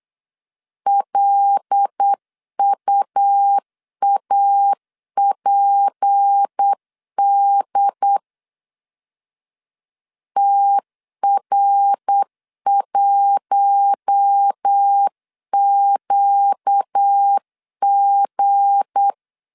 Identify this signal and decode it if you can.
Morse code, heard as a keyed tone, about 8 wpm, 800 Hz. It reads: LUAPD TR1QG
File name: LEZIONI / 7 GF9Z / 7 G 8wpm 3118